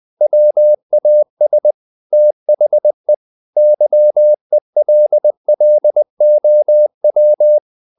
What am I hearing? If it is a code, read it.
Morse code, heard as a keyed tone, about 20 wpm, 600 Hz. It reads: WAS THE YELLOW
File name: the old man and the sea / the old man and the sea part 1 20wpm